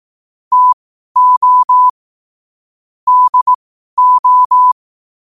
Morse code, heard as a keyed tone, about 18 wpm, spaced as for 12 wpm, 1000 Hz. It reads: TO DO
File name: qotdfiles / 12wpm